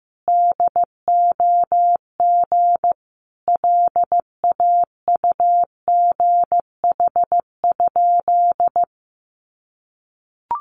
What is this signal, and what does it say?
Morse code, heard as a keyed tone, about 14 wpm, 700 Hz. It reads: DOG LAUGH?
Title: Sentences from Top 500 Words 15wpm